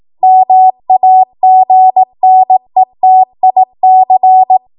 Morse code, heard as a keyed tone, about 18 wpm, 750 Hz. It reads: MAGNETIC
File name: Morse / 18WPM